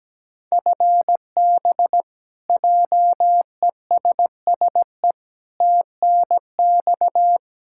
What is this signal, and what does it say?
Morse code, heard as a keyed tone, about 17 wpm, 700 Hz. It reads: FB JESSE TNX
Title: K1USN SST QSO Practice 17wpm